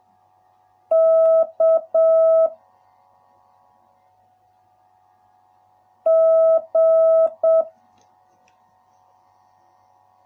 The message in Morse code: K G

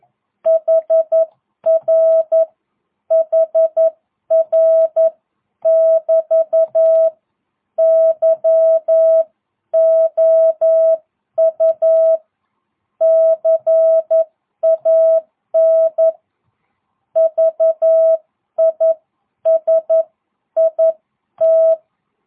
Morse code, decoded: HR HR <BT> YOU CAN VISIT

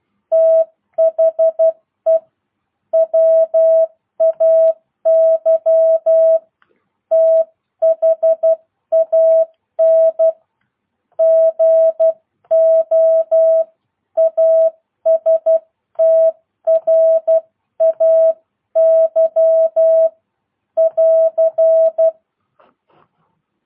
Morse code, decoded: THE WAY THAN GO ASTRAY <AR>

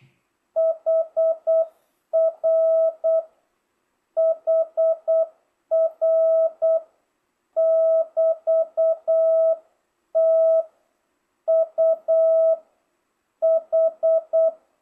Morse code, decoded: HR HR <BT> T U H

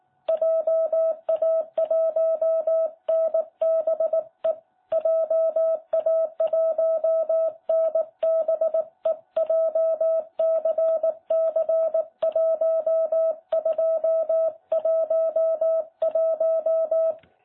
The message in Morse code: JA1NBE JA1NBE JCC1211

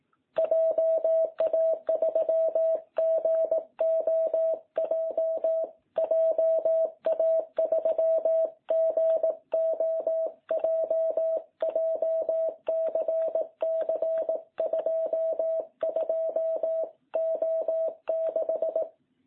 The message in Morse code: JA3GOJ JA3GOJ JCC 22O6

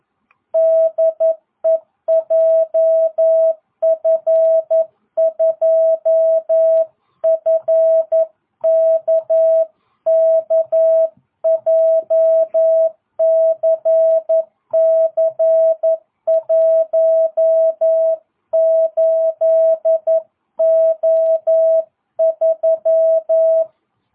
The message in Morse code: DE JF2FKK JCC18O3

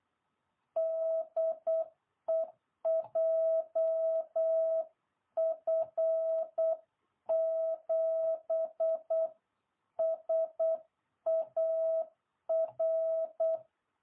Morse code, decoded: DE JF7SAR

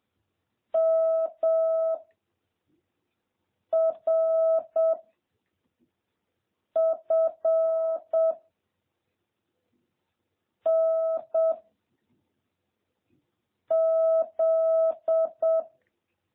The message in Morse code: M R F N Z

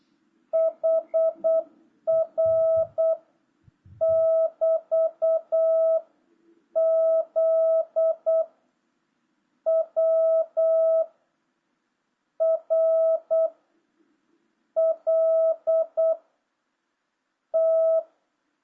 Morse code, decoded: HR <BT> Z W R L T

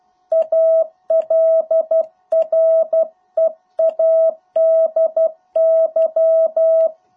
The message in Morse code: ALREADY

